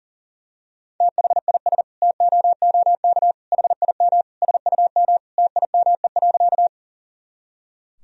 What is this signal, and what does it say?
Morse code, heard as a keyed tone, about 37 wpm, 700 Hz. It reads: THIS TOOK HIM SUM TIME.
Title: Sentences from Top 300 Words and Abbreviations 40wpm